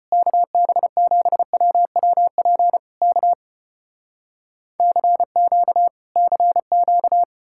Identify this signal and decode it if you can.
Morse code, heard as a keyed tone, about 35 wpm, 700 Hz. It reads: KB7WWP K CQ CQ